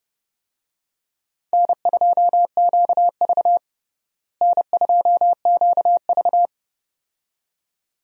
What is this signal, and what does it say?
Morse code, heard as a keyed tone, about 30 wpm, 700 Hz. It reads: N2QV N2QV
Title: USA Call Signs - 4 Characters - Repeated 2x - 30wpm